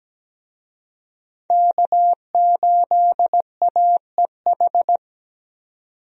Morse code, heard as a keyed tone, about 17 wpm, 700 Hz. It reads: K8AEH
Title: USA Call Signs - 5 Characters - 17wpm